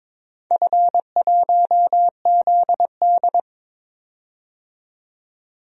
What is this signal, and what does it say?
Morse code, heard as a keyed tone, about 21 wpm, 700 Hz. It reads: F1ZD
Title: DXCC-100 Call Signs - 4 Characters - 22wpm